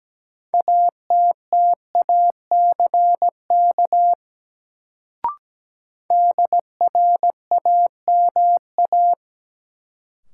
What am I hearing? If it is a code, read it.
Morse code, heard as a keyed tone, about 17 wpm, 700 Hz. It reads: ATTACK DRAMA
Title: Sentences from Top 1500 Words - New Word Drill Alt - 17wpm